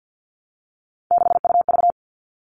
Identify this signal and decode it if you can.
Morse code, heard as a keyed tone, about 48 wpm, 700 Hz. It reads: 6UV
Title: All Cut Numbers - 3-Digit 50wpm